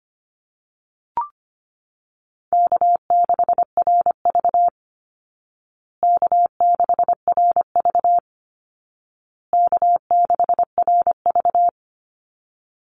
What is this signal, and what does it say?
Morse code, heard as a keyed tone, about 25 wpm, 700 Hz. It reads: K6RV K6RV K6RV
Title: Call Signs Repeated 2x - 25wpm